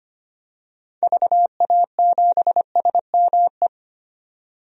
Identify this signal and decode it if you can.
Morse code, heard as a keyed tone, about 25 wpm, 700 Hz. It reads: VA7SME
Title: DXCC-100 Call Signs - 6 Characters - 25wpm